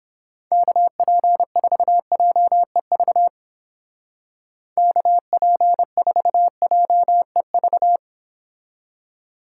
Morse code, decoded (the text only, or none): KP4JEV KP4JEV